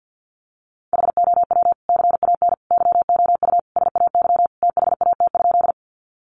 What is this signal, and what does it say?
Morse code, heard as a keyed tone, about 50 wpm, 700 Hz. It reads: HOW CAN YOU SAY THAT?
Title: Sentences from Top 200 Words 50wpm